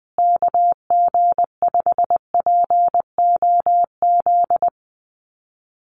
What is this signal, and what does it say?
Morse code, heard as a keyed tone, about 20 wpm, 700 Hz. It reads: KG5POZ
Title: USA Call Signs Repeated 2x 20wpm